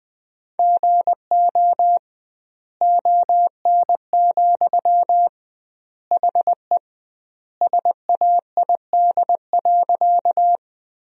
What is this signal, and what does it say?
Morse code, heard as a keyed tone, about 21 wpm, 700 Hz. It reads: GO ON, HE SAID.
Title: 2x Word-Spacing - Sentences from Top 100 Words - 20wpm